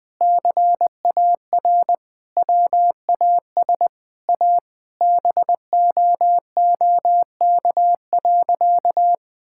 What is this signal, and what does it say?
Morse code, heard as a keyed tone, about 20 wpm, 700 Hz. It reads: CAR WAS A BOOK.
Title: Sentences with 4-Word Phrases - 20wpm